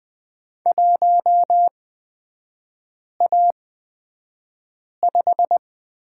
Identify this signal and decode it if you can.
Morse code, heard as a keyed tone, about 20 wpm, spaced as for 5 wpm, 700 Hz. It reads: 1A5
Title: 243 Lesson 39e - Letter z - Three Characters 20wpm